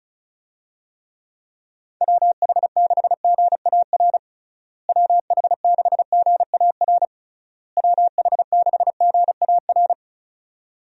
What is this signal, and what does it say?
Morse code, heard as a keyed tone, about 34 wpm, 700 Hz. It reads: WH6GAR WH6GAR WH6GAR
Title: USA Call Signs - 6 Characters - Repeated 3x - 35wpm